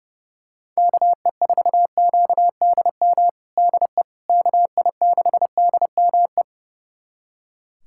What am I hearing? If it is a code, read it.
Morse code, heard as a keyed tone, about 32 wpm, 700 Hz. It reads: KE4QDM DE KI6DME
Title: USA Callsign DE Callsign - 6 Characters - 30wpm